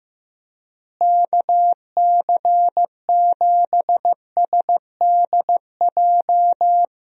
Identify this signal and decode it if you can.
Morse code, heard as a keyed tone, about 15 wpm, 700 Hz. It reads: KC7SDJ
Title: USA Call Signs - 6 Characters - Repeated 3x - 15wpm